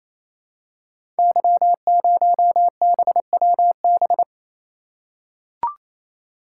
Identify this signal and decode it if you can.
Morse code, heard as a keyed tone, about 29 wpm, 700 Hz. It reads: Y0BWB
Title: Code Groups - 5 Letter:Numbers - 28wpm